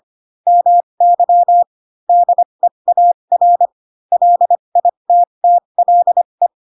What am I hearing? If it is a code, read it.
Morse code, heard as a keyed tone, about 25 wpm, spaced as for 22 wpm, 700 Hz. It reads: MY DEAR LITTLE
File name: Pinocchio / Pinocchio ch7 18 25 ik1hge